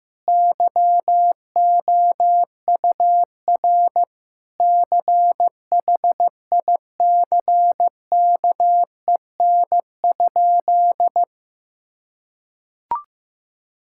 Morse code, heard as a keyed tone, about 15 wpm, 700 Hz. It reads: YOUR CHICKEN?
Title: Sentences from Top 1000 Words - Review Part 5 - 15wpm